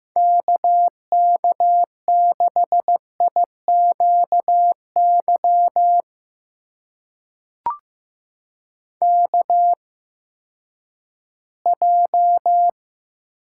Morse code, heard as a keyed tone, about 15 wpm, spaced as for 10 wpm, 700 Hz. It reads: KK6IQY KJ
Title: USA Call Signs - 6 Characters - 8x Character Spacing - 15wpm